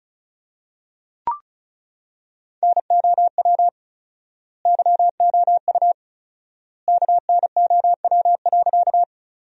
Morse code, heard as a keyed tone, about 36 wpm, 700 Hz. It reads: NOW YOU KNOW.
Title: 4x Word-Spacing - Sentences from Top 100 Words - 35wpm